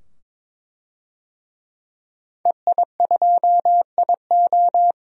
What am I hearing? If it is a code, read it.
Morse code, heard as a keyed tone, about 21 wpm, 700 Hz. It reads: EI2IO
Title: DXCC-100 Call Signs - 5 Characters - 22wpm